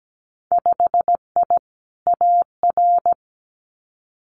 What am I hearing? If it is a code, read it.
Morse code, heard as a keyed tone, about 17 wpm, 700 Hz. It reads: 5I AR
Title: Winter Field Day Exchanges Encore 17wpm